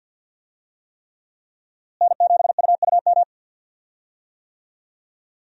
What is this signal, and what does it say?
Morse code, heard as a keyed tone, about 47 wpm, 700 Hz. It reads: N7UUM
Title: USA Call Signs - 5 Characters - Repeated 3x - 50wpm